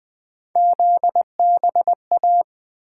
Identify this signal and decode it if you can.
Morse code, heard as a keyed tone, about 20 wpm, 700 Hz. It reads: ZBA
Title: Three Letter-Number Combinations - 20wpm